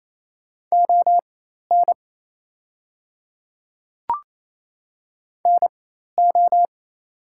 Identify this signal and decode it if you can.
Morse code, heard as a keyed tone, about 28 wpm, spaced as for 13 wpm, 690 Hz. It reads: ON NO
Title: 4x Character Spacing - Top 100 Words - 28wpm